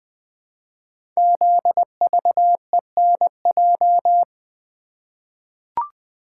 Morse code, heard as a keyed tone, about 20 wpm, 700 Hz. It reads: ZVENJ
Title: Code Groups - 5 Letters - 20wpm